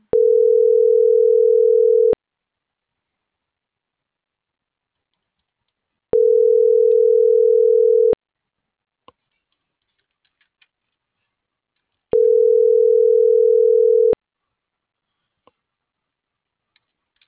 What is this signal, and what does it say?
Ringback tone (North America), 440 + 480 Hz.